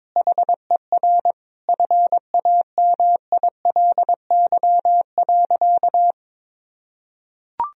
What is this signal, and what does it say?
Morse code, heard as a keyed tone, about 22 wpm, 700 Hz. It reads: HER FAMILY.
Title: Sentences from Top 1000 Words - Review Part 4 - 22wpm